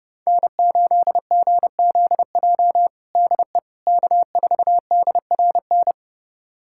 Morse code, heard as a keyed tone, about 30 wpm, 700 Hz. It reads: N8GZJ DE K4DRN
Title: USA Callsign DE Callsign 30wpm